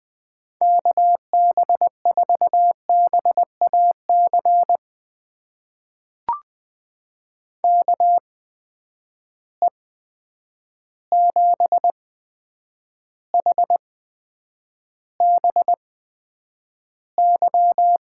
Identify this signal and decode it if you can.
Morse code, heard as a keyed tone, about 20 wpm, spaced as for 9 wpm, 700 Hz. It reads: KB4BAC KE7HBY
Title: USA Call Signs - 6 Characters - 8x Character Spacing - 20wpm